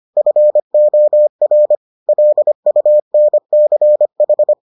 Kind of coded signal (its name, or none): Morse code